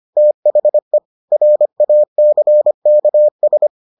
Morse code, heard as a keyed tone, about 25 wpm, 600 Hz. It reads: THE RACKS